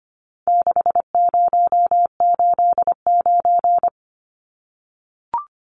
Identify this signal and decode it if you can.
Morse code, heard as a keyed tone, about 25 wpm, 700 Hz. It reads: 6089
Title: Numbers - 4-Digit 25wpm